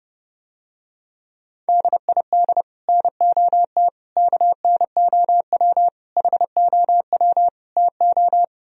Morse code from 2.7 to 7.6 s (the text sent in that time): NOT KNOW HOW